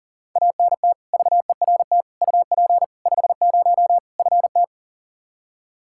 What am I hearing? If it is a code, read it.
Morse code, heard as a keyed tone, about 40 wpm, 700 Hz. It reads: ANT VERT UP 50 FT
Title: Practice QSO - Exchange 2 - Info - 40wpm